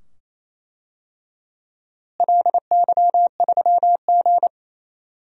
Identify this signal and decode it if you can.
Morse code, heard as a keyed tone, about 28 wpm, 700 Hz. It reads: LY3G
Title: DXCC-100 Call Signs - 4 Characters - Repeated 2x - 28wpm